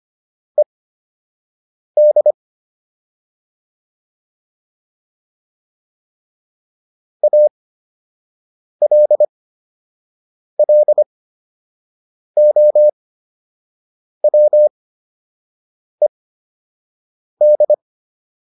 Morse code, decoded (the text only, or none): ED ALLOWED